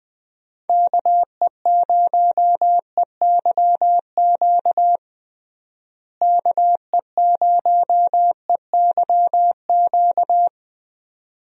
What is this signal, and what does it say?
Morse code, heard as a keyed tone, about 20 wpm, 700 Hz. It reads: KE0EYQ KE0EYQ
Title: USA Call Signs - 6 Characters - Repeated 2x - 20wpm